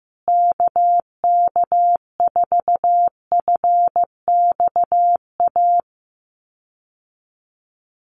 Morse code, heard as a keyed tone, about 15 wpm, 700 Hz. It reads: KK4FXA
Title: USA Call Signs Repeated 2x 15wpm